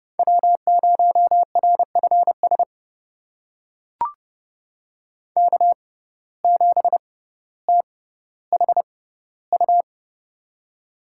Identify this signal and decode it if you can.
Morse code, heard as a keyed tone, about 30 wpm, spaced as for 15 wpm, 700 Hz. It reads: W0RFS K7THU
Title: USA Call Signs - 5 Characters - 6x Character Spacing - 30wpm